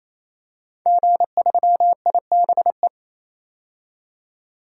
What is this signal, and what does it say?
Morse code, heard as a keyed tone, about 28 wpm, 700 Hz. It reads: G3IBE
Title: DXCC-100 Call Signs - 5 Characters - 28wpm